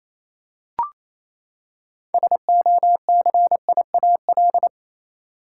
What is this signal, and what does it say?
Morse code, heard as a keyed tone, about 28 wpm, 700 Hz. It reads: SOCIAL
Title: Practice QSO - Exchange 3 - More Info - 28wpm